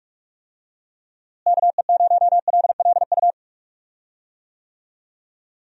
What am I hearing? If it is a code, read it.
Morse code, heard as a keyed tone, about 46 wpm, 700 Hz. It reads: KE0LLU
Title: USA Call Signs - 6 Characters - Repeated 3x - 45wpm